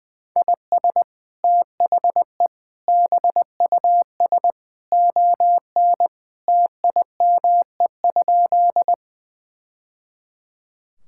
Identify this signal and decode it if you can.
Morse code, heard as a keyed tone, about 20 wpm, 700 Hz. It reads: IS THE BUS ON TIME?
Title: Sentences from Top 1000 Words - Review Part 6 - 20wpm